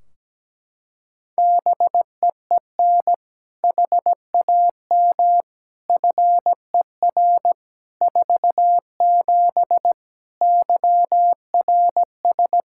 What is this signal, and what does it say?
Morse code, heard as a keyed tone, about 17 wpm, 700 Hz. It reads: BEEN HAM FER 47 YRS
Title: Practice QSO - Exchange 3 - More Info - 17wpm